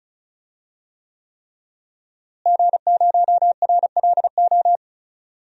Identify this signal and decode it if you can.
Morse code, heard as a keyed tone, about 34 wpm, 700 Hz. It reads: G0RLO
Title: DXCC-100 Call Signs - 5 Characters - Repeated 3x - 35wpm